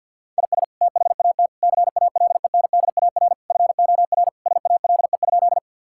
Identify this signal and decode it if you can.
Morse code, heard as a keyed tone, about 51 wpm, 700 Hz. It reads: IS THAT CALENDAR FOR SALE?
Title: Sentences from Top 1600 Words 50wpm